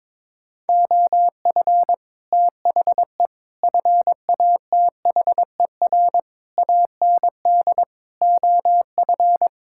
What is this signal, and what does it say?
Morse code, heard as a keyed tone, about 22 wpm, 700 Hz. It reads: OF THE FATHER AND OF
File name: Sentences with 9-Word Phrases - 22wpm